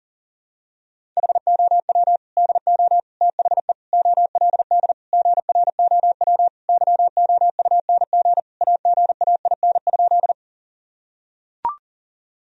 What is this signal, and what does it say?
Morse code, heard as a keyed tone, about 40 wpm, 700 Hz. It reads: HOW DO THE OLD GROW YOUNG AGAIN?
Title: Sentences from Top 1000 Words - Review Part 1 - 40wpm